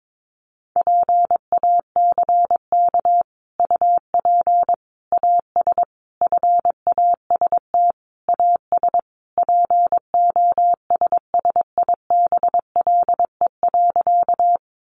Morse code, heard as a keyed tone, about 22 wpm, 700 Hz. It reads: PACK UP AS FAST AS POSSIBLE.
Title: Sentences from Top 1300 Words 22wpm